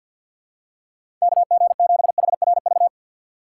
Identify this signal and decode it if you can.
Morse code, heard as a keyed tone, about 53 wpm, 700 Hz. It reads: KG7HRV
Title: USA Call Signs - 4x Character Spacing - 50wpm